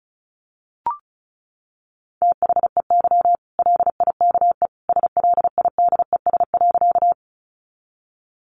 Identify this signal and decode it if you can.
Morse code, heard as a keyed tone, about 34 wpm, 700 Hz. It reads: THEY LIKE SLIDES.
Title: Easy Phrases 35wpm